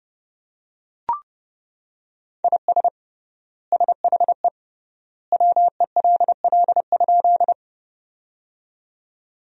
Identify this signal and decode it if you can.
Morse code, heard as a keyed tone, about 31 wpm, 700 Hz. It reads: IS SHE WELL?
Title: 3x Word-Spacing - Sentences from Top 200 Words - 30wpm